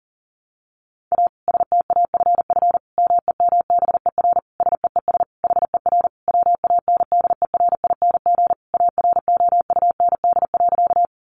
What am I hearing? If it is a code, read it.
Morse code, heard as a keyed tone, about 40 wpm, 700 Hz. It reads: A STAFF MEMBER SEES HER WANDERING AROUND.
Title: Additional Easy Phrases 40wpm